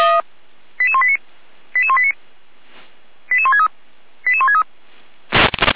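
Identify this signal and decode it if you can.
Five-tone selective call (ZVEI), heard as quick runs of five tones, tones of about 70 ms. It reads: 88189 88189 88163 88163
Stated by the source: Five-tone ZVEI selective call 88505, received by SDR.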